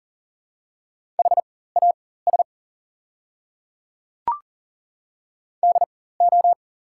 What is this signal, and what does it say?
Morse code, heard as a keyed tone, about 40 wpm, spaced as for 19 wpm, 690 Hz. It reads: HAS DO
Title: 4x Character Spacing - Top 100 Words - 40wpm